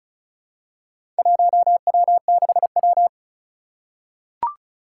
Morse code, heard as a keyed tone, about 34 wpm, 700 Hz. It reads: 1W6W
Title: Four Challenging Letter-Number Combinations 35wpm